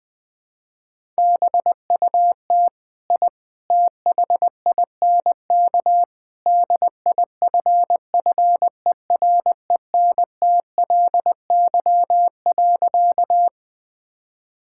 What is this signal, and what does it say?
Morse code, heard as a keyed tone, about 20 wpm, 700 Hz. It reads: BUT I THINK DIFFERENTLY.
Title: Sentences with 3-Word Phrases - 20wpm